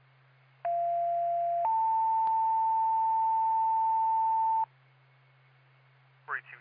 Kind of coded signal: two-tone page